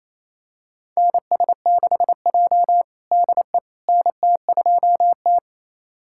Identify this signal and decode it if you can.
Morse code, heard as a keyed tone, about 28 wpm, 700 Hz. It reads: NS6J DE NT2T
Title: USA Callsign DE Callsign - 4 Characters - 28wpm